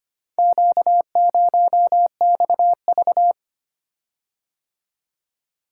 Morse code, heard as a keyed tone, about 25 wpm, 700 Hz. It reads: Q0XV
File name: practice / 25wpm / Four Letter-Number Combinations - 25wpm